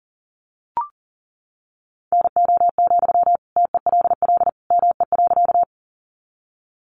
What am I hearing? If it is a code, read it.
Morse code, heard as a keyed tone, about 40 wpm, 700 Hz. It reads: NO, TELL ME.
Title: Sentences from Top 200 Words 40wpm